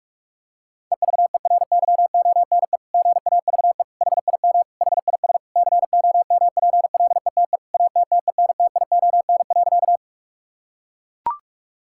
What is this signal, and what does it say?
Morse code, heard as a keyed tone, about 41 wpm, 700 Hz. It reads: EVERYONE GAVE HIM HIS COMPLETE ATTENTION.
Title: Sentences from Top 1000 Words - Review Part 6 - 45wpm